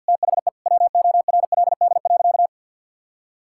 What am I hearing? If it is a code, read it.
Morse code, heard as a keyed tone, about 50 wpm, 700 Hz. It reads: THE WORLD.
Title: Sentences from Top 1000 Words - Review Part 6 - 50wpm